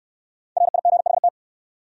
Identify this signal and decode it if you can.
Morse code, heard as a keyed tone, about 62 wpm, 700 Hz. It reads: RIGHT